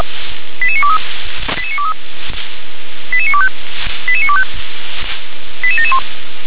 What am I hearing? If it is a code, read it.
Five-tone selective call (ZVEI), heard as quick runs of five tones, tones of about 70 ms. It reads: 88923 88923 88925 88925 88771